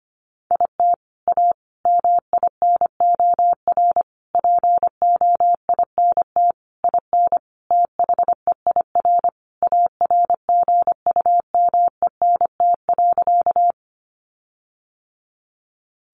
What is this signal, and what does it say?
Morse code, heard as a keyed tone, about 25 wpm, 700 Hz. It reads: IT A MINOR POINT IN THEIR ARGUMENT.
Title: Sentences from Top 1500 Words 25wpm